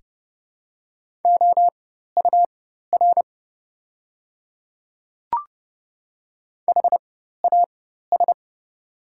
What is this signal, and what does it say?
Morse code, heard as a keyed tone, about 30 wpm, spaced as for 14 wpm, 700 Hz. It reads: OUR HAS
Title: 4x Character Spacing - Top 300 Words - 30wpm